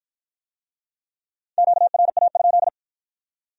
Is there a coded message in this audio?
Yes: Morse code from 1.6 to 2.7 s, heard as a keyed tone, about 46 wpm, 700 Hz. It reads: QRA?